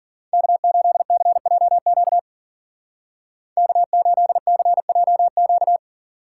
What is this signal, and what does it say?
Morse code, heard as a keyed tone, about 53 wpm, 700 Hz. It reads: K8CJQ K8CJQ